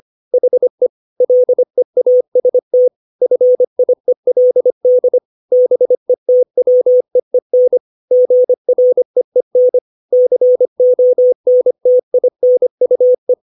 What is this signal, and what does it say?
Morse code, heard as a keyed tone, about 25 wpm, 500 Hz. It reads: HE LEAST FIELD BETWEEN GREEN CONTINUE
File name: 632 words / 632 Words 25WPM